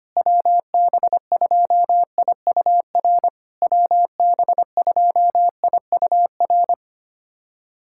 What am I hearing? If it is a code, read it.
Morse code, heard as a keyed tone, about 25 wpm, 700 Hz. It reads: WB2IUR WB2IUR
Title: SOTA QSO Practice 25wpm